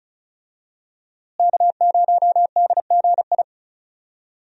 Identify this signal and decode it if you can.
Morse code, heard as a keyed tone, about 34 wpm, 700 Hz. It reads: K0DGI
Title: USA Call Signs - 5 Characters - Repeated 2x - 35wpm